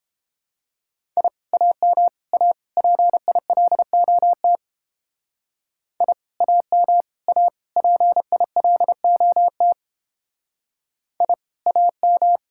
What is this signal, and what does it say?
Morse code, heard as a keyed tone, about 29 wpm, 700 Hz. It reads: I AM A PILOT I AM A PILOT I AM